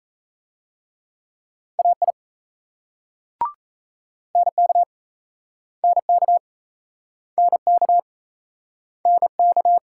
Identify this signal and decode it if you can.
Morse code, heard as a keyed tone, about 36 wpm, 700 Hz. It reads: AI NK NK NK NK